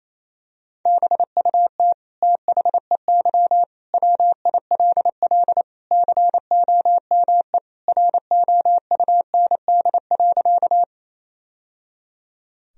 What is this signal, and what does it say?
Morse code, heard as a keyed tone, about 28 wpm, 700 Hz. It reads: BUT THEY WILL COME ROUND.
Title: Sentences from Top 1000 Words - Review Part 1 - 28wpm